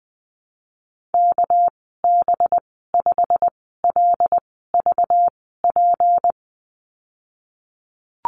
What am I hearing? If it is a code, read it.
Morse code, heard as a keyed tone, about 20 wpm, spaced as for 15 wpm, 700 Hz. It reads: KB5LVP
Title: 200 Lesson 32g - Letter k - Call Signs 20wpm